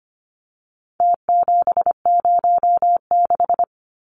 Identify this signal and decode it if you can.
Morse code, heard as a keyed tone, about 25 wpm, 700 Hz. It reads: T706
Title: Numbers - Contest Style - 4-Digit 25wpm